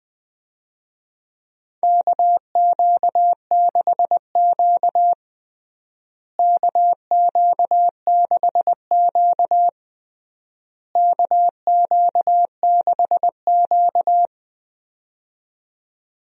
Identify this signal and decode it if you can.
Morse code, heard as a keyed tone, about 21 wpm, 700 Hz. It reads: KQ6Q KQ6Q KQ6Q
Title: USA Call Signs - 4 Characters - Repeated 3x - 20wpm